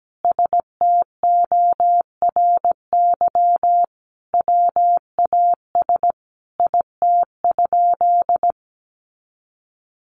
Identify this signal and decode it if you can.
Morse code, heard as a keyed tone, about 17 wpm, 700 Hz. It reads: STORY WAS IT?
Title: Sentences from Top 300 Words 17wpm